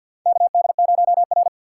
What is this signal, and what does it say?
Morse code, heard as a keyed tone, about 51 wpm, 700 Hz. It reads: KD0R
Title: USA Call Signs - 6x Character Spacing - 50wpm